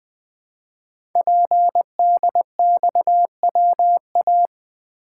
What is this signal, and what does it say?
Morse code, heard as a keyed tone, about 20 wpm, 700 Hz. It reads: PDXWA
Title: Code Groups - 5 Letter:Numbers - 20wpm